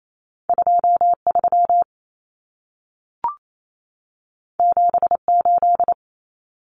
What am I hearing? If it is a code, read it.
Morse code, heard as a keyed tone, about 28 wpm, 700 Hz. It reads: 23 78
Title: Common Cut Numbers - 2-Digit 28wpm